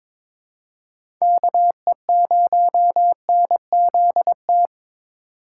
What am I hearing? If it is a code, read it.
Morse code, heard as a keyed tone, about 22 wpm, 700 Hz. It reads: KE0NZT
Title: USA Call Signs - 6 Characters - 3x Character Spacing - 22wpm